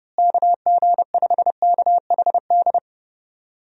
Morse code, heard as a keyed tone, about 29 wpm, 700 Hz. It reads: KG5KHD
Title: USA Call Signs - 6 Characters - Repeated 3x - 30wpm